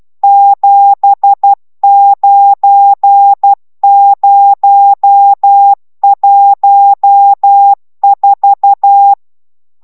Morse code, heard as a keyed tone, about 12 wpm, 800 Hz. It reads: 79014